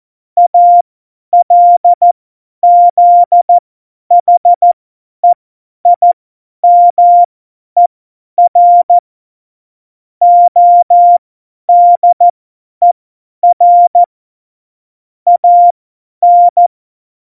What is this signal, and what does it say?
Morse code, heard as a keyed tone, about 14 wpm, spaced as for 10 wpm, 700 Hz. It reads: ALZHEIMER ODER AN
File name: Morsekurs / Witze / Witz 01. Oktober 2019, 50 F70